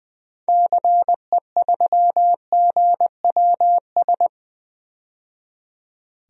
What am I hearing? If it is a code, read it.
Morse code, heard as a keyed tone, about 20 wpm, 700 Hz. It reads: CE3GWS
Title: DXCC-100 Call Signs - 6 Characters - 20wpm